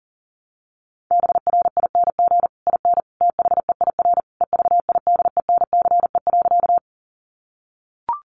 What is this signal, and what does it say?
Morse code, heard as a keyed tone, about 40 wpm, 700 Hz. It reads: BRING IN THEIR EVIDENCE.
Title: Sentences from Top 1300 Words 40wpm